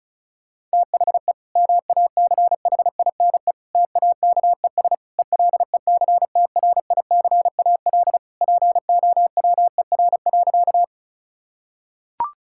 Morse code, heard as a keyed tone, about 36 wpm, 700 Hz. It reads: THE MACHINE TAKES ELECTRICAL POWER.
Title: Sentences from Top 1000 Words - Review Part 6 - 35wpm